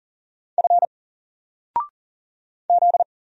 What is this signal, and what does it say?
Morse code, heard as a keyed tone, about 39 wpm, 700 Hz. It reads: F Z
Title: Single Character (Letter:Number:Punctuation:Prosign) 40wpm